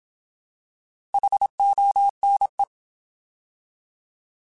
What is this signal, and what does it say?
Morse code, heard as a keyed tone, about 27 wpm, 800 Hz. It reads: HONE
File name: Alphabetical / Alphabetical mp3s / Hs @25